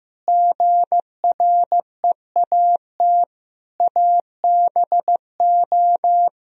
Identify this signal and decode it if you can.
Morse code, heard as a keyed tone, about 15 wpm, 700 Hz. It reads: GREAT ABO
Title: Sentences from Top 1000 Words - Review Part 1 - 15wpm